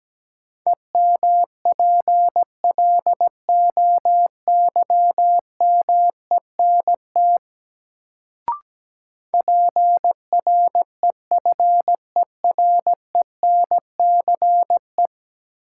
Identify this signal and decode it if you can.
Morse code, heard as a keyed tone, about 17 wpm, 700 Hz. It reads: EMPLOYMENT PREFERENCE
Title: Sentences from Top 1500 Words - New Word Drill Alt - 17wpm